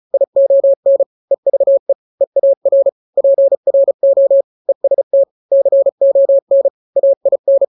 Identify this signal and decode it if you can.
Morse code, heard as a keyed tone, about 34 wpm, 550 Hz. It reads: ION EVE EAR PRO EST CON AIN
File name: 3-grams 35 30 WPM